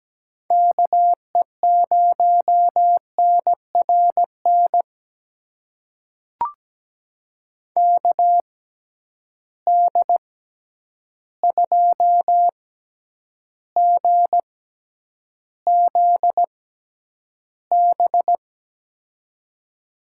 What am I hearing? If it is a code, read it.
Morse code, heard as a keyed tone, about 17 wpm, spaced as for 9 wpm, 700 Hz. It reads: KE0NRN KD2GZB
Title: USA Call Signs - 6x Character Spacing - 17wpm